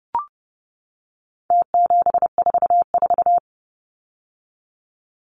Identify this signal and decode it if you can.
Morse code, heard as a keyed tone, about 28 wpm, 700 Hz. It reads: T744
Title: Numbers - Contest Style - 4-Digit 30wpm